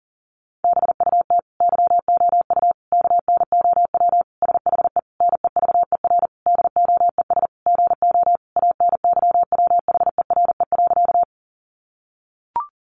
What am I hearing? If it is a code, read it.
Morse code, heard as a keyed tone, about 40 wpm, 700 Hz. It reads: BUT YOU KNOW SHE NEVER DOES GO ANYWHERE.
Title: Sentences from Top 1400 Words 40wpm